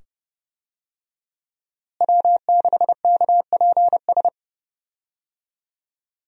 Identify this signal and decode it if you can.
Morse code, heard as a keyed tone, about 30 wpm, 700 Hz. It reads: W6KPS
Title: USA Call Signs - 5 Characters - Repeated 2x - 30wpm